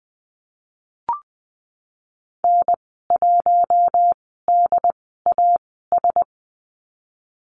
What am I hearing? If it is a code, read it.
Morse code, heard as a keyed tone, about 20 wpm, spaced as for 15 wpm, 700 Hz. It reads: N1DAS
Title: 088 Lesson 15g - Number 5 - Call Signs 20wpm